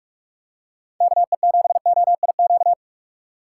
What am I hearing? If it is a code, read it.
Morse code, heard as a keyed tone, about 44 wpm, 700 Hz. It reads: KE7OIQ
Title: USA Call Signs - 6 Characters - 6x Character Spacing - 45wpm